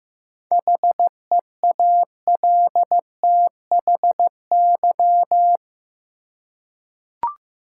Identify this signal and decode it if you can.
Morse code, heard as a keyed tone, about 15 wpm, 700 Hz. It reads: HEALTHY
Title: Seven Letter Words - 15wpm